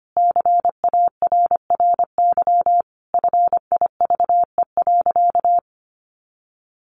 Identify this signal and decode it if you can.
Morse code, heard as a keyed tone, about 25 wpm, 700 Hz. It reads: CARRY FIVE.
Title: Sentences from Top 400 Words 25wpm